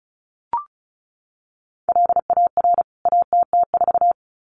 Morse code, heard as a keyed tone, about 34 wpm, 700 Hz. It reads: LAR ATT4
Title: CWT Exchange 35wpm